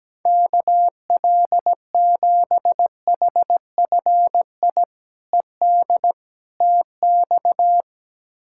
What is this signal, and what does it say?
Morse code, heard as a keyed tone, about 17 wpm, 700 Hz. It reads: KL7HFI ED TX
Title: K1USN SST QSO Practice - 17wpm